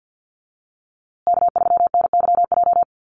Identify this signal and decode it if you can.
Morse code, heard as a keyed tone, about 48 wpm, 700 Hz. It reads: K3NYJ
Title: USA Call Signs Repeated 3x 50wpm